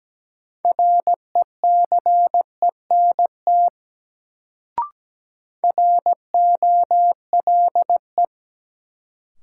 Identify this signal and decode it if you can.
Morse code, heard as a keyed tone, about 17 wpm, 700 Hz. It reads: RECENT ROLE
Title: Sets of 3 Words - New Word Drill - 17wpm